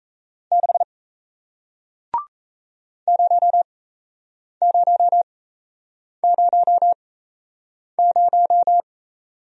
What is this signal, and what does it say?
Morse code, heard as a keyed tone, about 38 wpm, spaced as for 8 wpm, 700 Hz. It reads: 6 0000